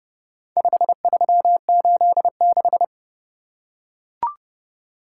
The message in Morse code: 5386

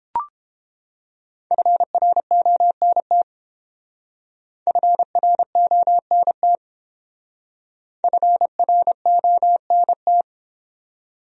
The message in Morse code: FRONT FRONT FRONT

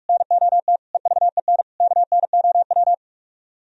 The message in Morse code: NOT EVEN KNOW

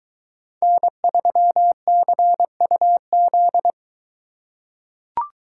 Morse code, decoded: N3CUZ

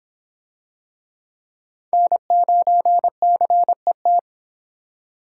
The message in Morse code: N9CET

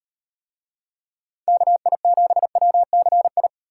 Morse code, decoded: KI7WCI

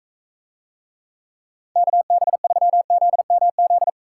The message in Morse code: KB3ZMZ

